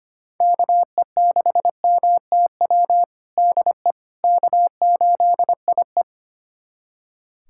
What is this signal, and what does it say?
Morse code, heard as a keyed tone, about 25 wpm, 700 Hz. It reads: KE6MTW DE K8IE
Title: USA Callsign DE Callsign 25wpm